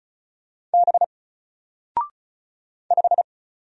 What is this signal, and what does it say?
Morse code, heard as a keyed tone, about 35 wpm, 700 Hz. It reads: B 5
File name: Single-Letter-Number-35wpm